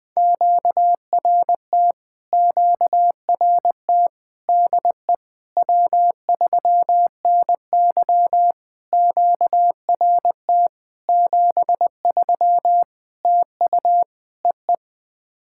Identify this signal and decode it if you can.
Morse code, heard as a keyed tone, about 20 wpm, 700 Hz. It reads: QRT QRT DE W3NY QRT 73 TU EE
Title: POTA QSO Practice - 20wpm